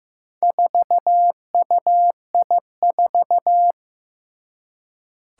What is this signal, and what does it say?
Morse code, heard as a keyed tone, about 15 wpm, 700 Hz. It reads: 4UI4
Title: Four Challenging Letter-Number Combinations 15wpm